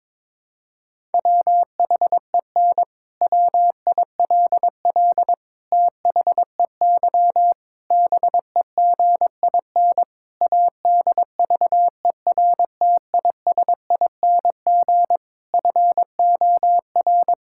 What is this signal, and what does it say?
Morse code, heard as a keyed tone, about 22 wpm, 700 Hz. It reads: WHEN WILL THEY BEGIN ADVERTISING FOR
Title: Sentences from Top 1000 Words - Review Part 6 - 22wpm